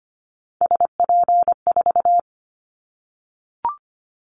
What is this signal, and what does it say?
Morse code, heard as a keyed tone, about 25 wpm, 700 Hz. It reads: SP4
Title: Callsign Prefixes for Top 25 DXCC - 25wpm